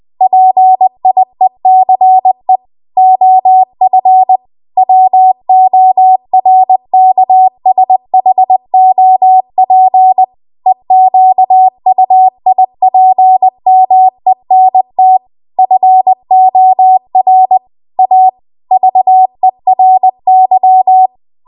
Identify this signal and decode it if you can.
Morse code, heard as a keyed tone, about 20 wpm, 750 Hz. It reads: PIECE OF WORKSHOP EQUIPMENT FOR A VERY
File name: Archive / 20 WPM / 20WPM